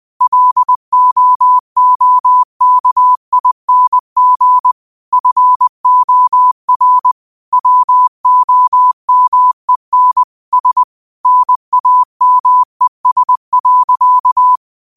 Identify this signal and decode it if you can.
Morse code, heard as a keyed tone, about 20 wpm, 1000 Hz. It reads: LOOKING FOR WOMEN S NAMES.
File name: qotdfiles / 20wpm